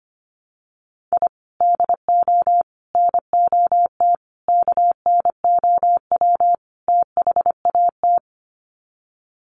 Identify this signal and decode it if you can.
Morse code, heard as a keyed tone, about 25 wpm, 700 Hz. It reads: I DO NOT KNOW THAT
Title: Sets of 5 Words 25wpm